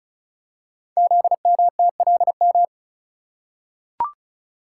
Morse code, heard as a keyed tone, about 36 wpm, 700 Hz. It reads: ZMTLM
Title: Code Groups - 5 Letters - 35wpm